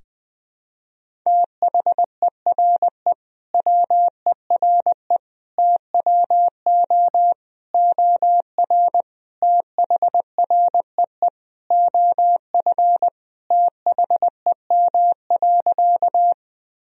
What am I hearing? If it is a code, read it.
Morse code, heard as a keyed tone, about 20 wpm, 700 Hz. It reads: THERE WERE TWO OR THREE OF THEM.
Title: Sentences from Top 1000 Words - Review Part 1 - 20wpm